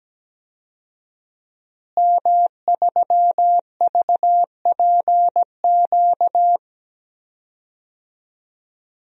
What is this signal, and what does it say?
Morse code, heard as a keyed tone, about 17 wpm, 700 Hz. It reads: M3VPQ
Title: DXCC-100 Call Signs - 5 Characters - Repeated 3x - 17wpm